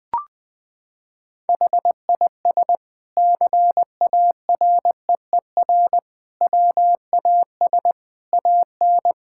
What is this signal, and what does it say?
Morse code, heard as a keyed tone, about 20 wpm, 700 Hz. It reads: HIS CAREER WAS AN
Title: Sentences from Top 1000 Words - Review Part 6 - 20wpm